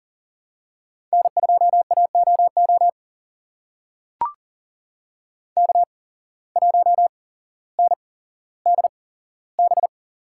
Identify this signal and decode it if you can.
Morse code, heard as a keyed tone, about 40 wpm, spaced as for 17 wpm, 700 Hz. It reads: N2AOO K1NDB
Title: USA Call Signs - 5 Characters - 8x Character Spacing - 40wpm